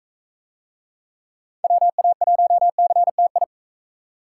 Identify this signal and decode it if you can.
Morse code, heard as a keyed tone, about 45 wpm, 700 Hz. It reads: WA1CTI